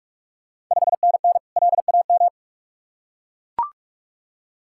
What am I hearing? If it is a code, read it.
Morse code, heard as a keyed tone, about 44 wpm, 690 Hz. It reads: 5NN LAM
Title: Ontario QSO Party Exchange 45wpm